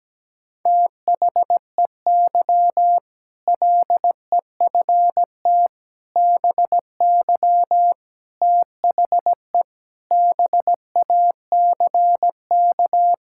Morse code, heard as a keyed tone, about 17 wpm, 700 Hz. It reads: THEY LEFT BY THE BACK
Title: Sentences from Top 1600 Words 17wpm